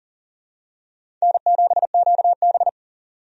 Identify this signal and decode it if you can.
Morse code, heard as a keyed tone, about 40 wpm, 700 Hz. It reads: N7QB
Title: USA Call Signs - 4 Characters - Repeated 3x - 40wpm